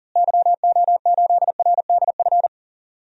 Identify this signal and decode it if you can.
Morse code, heard as a keyed tone, about 40 wpm, 700 Hz. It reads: YO8RDF
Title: DXCC-100 Call Signs - 6 Characters - Repeated 2x - 40wpm